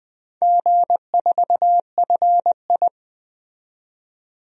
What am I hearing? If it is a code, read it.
Morse code, heard as a keyed tone, about 20 wpm, 700 Hz. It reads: G4FI
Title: DXCC-100 Call Signs - 4 Characters - 20wpm